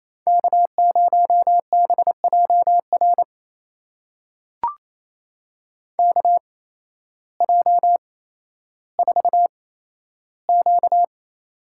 Morse code, heard as a keyed tone, about 27 wpm, spaced as for 13 wpm, 700 Hz. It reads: K0BJR KJ4Q
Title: USA Call Signs - 5 Characters - 8x Character Spacing - 28wpm